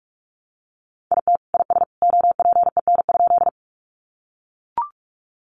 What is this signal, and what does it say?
Morse code, heard as a keyed tone, about 45 wpm, 700 Hz. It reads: IT IS OPEN?
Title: Sentences from Top 300 Words 45wpm